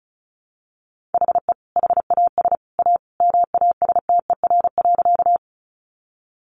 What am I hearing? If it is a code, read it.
Morse code, heard as a keyed tone, about 34 wpm, 700 Hz. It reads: HE HAS A MASTER.
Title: Sentences from Top 1300 Words 35wpm